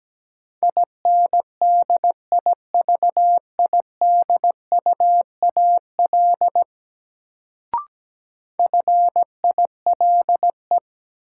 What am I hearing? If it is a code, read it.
Morse code, heard as a keyed tone, about 17 wpm, 700 Hz. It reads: INDIVIDUAL FILE
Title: Sentences from Top 1000 Words - Review Part 4 - New Word Drill - 17wpm